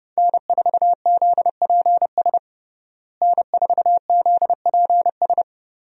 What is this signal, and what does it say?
Morse code, heard as a keyed tone, about 32 wpm, 700 Hz. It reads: N4ZPS N4ZPS
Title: USA Call Signs - 5 Characters - Repeated 3x - 30wpm